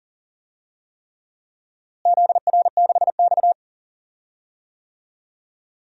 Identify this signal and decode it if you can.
Morse code, heard as a keyed tone, about 40 wpm, 700 Hz. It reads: ZR6X
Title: DXCC-100 Call Signs - 4 Characters - Repeated 3x - 40wpm